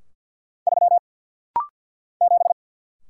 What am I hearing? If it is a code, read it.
Morse code, heard as a keyed tone, about 49 wpm, 690 Hz. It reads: 3 7
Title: Single Letter-Number - Rapid-Fire 50wpm